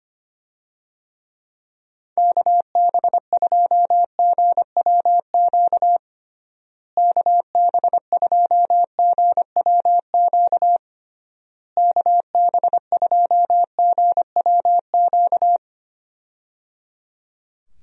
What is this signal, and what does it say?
Morse code, heard as a keyed tone, about 25 wpm, 700 Hz. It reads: KB2GWQ KB2GWQ KB2GWQ
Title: USA Call Signs - 6 Characters - Repeated 3x - 25wpm